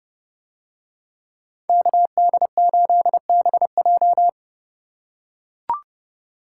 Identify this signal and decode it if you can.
Morse code, heard as a keyed tone, about 31 wpm, 700 Hz. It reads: KD8BJ